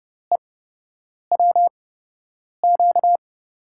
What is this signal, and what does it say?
Morse code, heard as a keyed tone, about 31 wpm, spaced as for 8 wpm, 700 Hz. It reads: EWQ
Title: USA Call Signs - 5 Characters - 8x Character Spacing - 30wpm